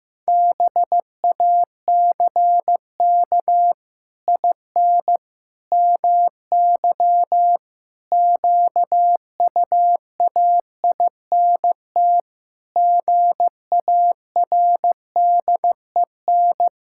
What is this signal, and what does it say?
Morse code, heard as a keyed tone, about 15 wpm, 700 Hz. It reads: BACK IN MY QUAINT GARDEN
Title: Pangram Challenge v2 - 15wpm